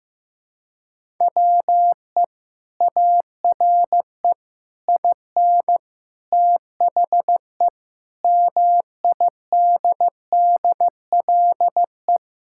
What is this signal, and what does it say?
Morse code, heard as a keyed tone, about 16 wpm, 700 Hz. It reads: WE ARE IN THE MIDDLE
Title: Sentences from Top 1000 Words - Review Part 6 - 15wpm